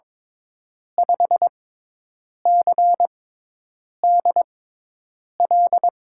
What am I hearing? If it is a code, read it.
Morse code, heard as a keyed tone, about 22 wpm, 700 Hz. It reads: 5 C D L
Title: USA Call Signs - 6x Character Spacing - 22wpm